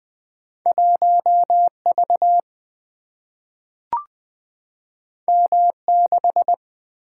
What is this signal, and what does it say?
Morse code, heard as a keyed tone, about 20 wpm, 700 Hz. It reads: 1 V M 6
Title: Single-Letter Single-Number 20wpm